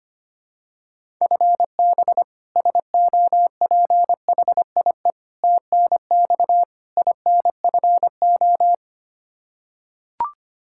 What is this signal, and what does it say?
Morse code, heard as a keyed tone, about 25 wpm, 700 Hz. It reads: FB SOPHIE TNX INFO
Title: Practice QSO - Exchange 3 - More Info - 25wpm